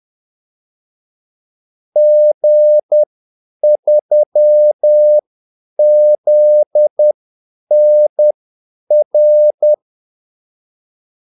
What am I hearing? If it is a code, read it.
Morse code, heard as a keyed tone, about 10 wpm, spaced as for 8 wpm, 600 Hz. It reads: G3ZNR
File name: Callsigns-10-8wpm